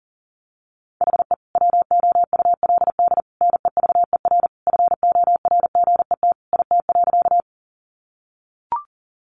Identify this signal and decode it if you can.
Morse code, heard as a keyed tone, about 37 wpm, 700 Hz. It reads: HE WOULD NEVER FORGET IT.
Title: Sentences from Top 1200 Words 40wpm